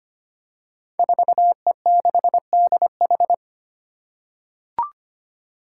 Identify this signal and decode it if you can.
Morse code, heard as a keyed tone, about 25 wpm, 700 Hz. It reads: 4E6DH
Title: Five Challenging Letter-Number Combinations 25wpm